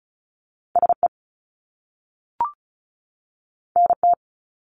Morse code, heard as a keyed tone, about 36 wpm, 700 Hz. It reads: SE NT
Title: Top Two Letter Combinations in Words 35wpm